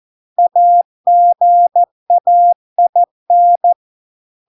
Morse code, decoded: AGAIN

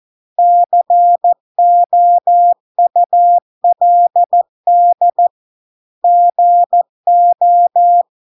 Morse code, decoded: COULD GO

